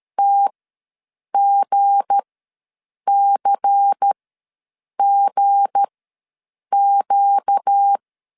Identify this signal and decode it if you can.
Morse code, heard as a keyed tone, about 13 wpm, spaced as for 7 wpm, 800 Hz. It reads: TGCGQ
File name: LEZIONI / 7 GF9Z / 7 G ws 12wpm 3140